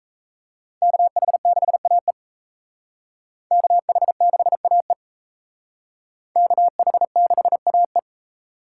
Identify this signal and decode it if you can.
Morse code, heard as a keyed tone, about 35 wpm, 700 Hz. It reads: KH6AE KH6AE KH6AE